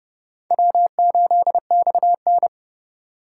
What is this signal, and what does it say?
Morse code, heard as a keyed tone, about 30 wpm, 700 Hz. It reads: W8XN
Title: USA Call Signs - 4 Characters - 2x Character Spacing - 30wpm